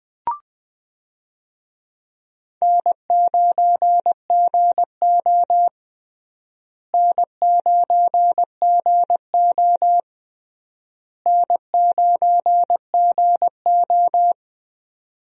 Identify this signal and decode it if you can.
Morse code, heard as a keyed tone, about 20 wpm, 700 Hz. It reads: N9GO N9GO N9GO
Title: USA Call Signs - 4 Characters - Repeated 3x - 20wpm